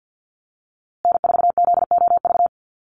Morse code, heard as a keyed tone, about 50 wpm, 700 Hz. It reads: N4ZOV
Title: USA Call Signs Repeated 3x 50wpm